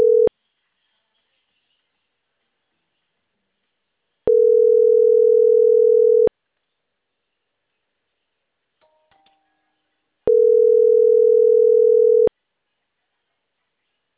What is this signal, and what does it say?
Ringback tone (North America), 440 + 480 Hz.